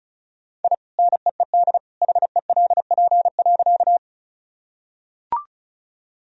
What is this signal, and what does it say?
Morse code, heard as a keyed tone, about 35 wpm, 700 Hz. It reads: I NEED HELP.
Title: Sentences from Top 1000 Words - Review Part 1 - 35wpm